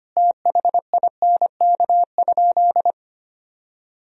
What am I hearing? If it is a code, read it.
Morse code, heard as a keyed tone, about 25 wpm, 700 Hz. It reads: THINK?
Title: Sentences from Top 200 Words and Abbreviations 25wpm